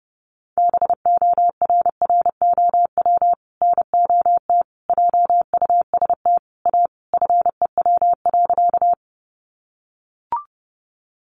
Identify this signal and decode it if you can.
Morse code, heard as a keyed tone, about 30 wpm, 700 Hz. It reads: BORROW NOT JUST A FEW.
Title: Sentences from Top 1500 Words 30wpm